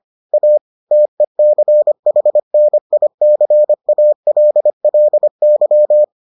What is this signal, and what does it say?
Morse code, heard as a keyed tone, about 25 wpm, 600 Hz. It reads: A TECHNICALLY